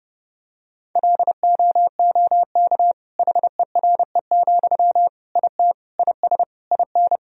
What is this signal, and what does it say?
Morse code, heard as a keyed tone, about 32 wpm, 700 Hz. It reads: LOOK HERE, IT IS IN
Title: Sentences from Top 1000 Words - Review Part 2 - 30wpm